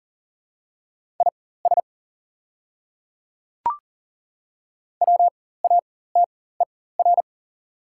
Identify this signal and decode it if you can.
Morse code, heard as a keyed tone, about 40 wpm, spaced as for 19 wpm, 690 Hz. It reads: IS WATER
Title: 4x Character Spacing - Top 300 Words - 40wpm